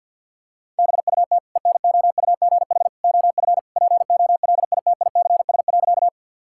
Morse code, heard as a keyed tone, about 52 wpm, 700 Hz. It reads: BUT ENOUGH OF POLITICS.
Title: Sentences from Top 1000 Words - Review Part 6 - 50wpm